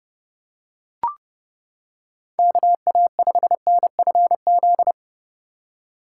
Morse code, heard as a keyed tone, about 30 wpm, 700 Hz. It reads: KA5NFZ